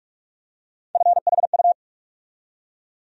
Morse code, heard as a keyed tone, about 47 wpm, 700 Hz. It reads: FHU
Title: USA Call Sign Postfixes - 3 Letters 45wpm